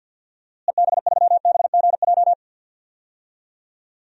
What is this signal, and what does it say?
Morse code, heard as a keyed tone, about 48 wpm, 700 Hz. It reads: EB3BGJ
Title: DXCC-100 Call Signs - 6 Characters - Repeated 3x - 50wpm